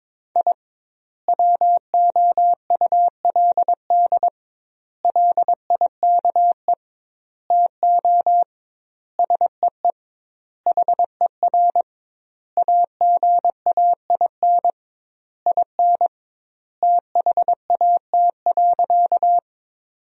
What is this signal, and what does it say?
Morse code, heard as a keyed tone, about 22 wpm, 700 Hz. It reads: I WOULD LIKE TO SEE HER AGAIN IN THAT.
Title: 2x Word-Spacing - Sentences from Top 200 Words - 22wpm